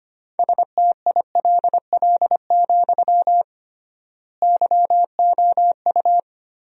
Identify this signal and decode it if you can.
Morse code, heard as a keyed tone, about 25 wpm, 700 Hz. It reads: STILL, YOU
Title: 3x Word-Spacing - Sentences from Top 300 Words - 25wpm